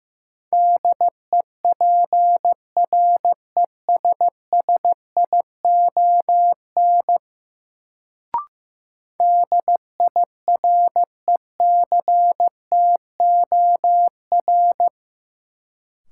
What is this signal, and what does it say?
Morse code, heard as a keyed tone, about 15 wpm, 700 Hz. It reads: DEPRESSION DIRECTOR
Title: Sentences from Top 1300 Words - New Word Drill - 15wpm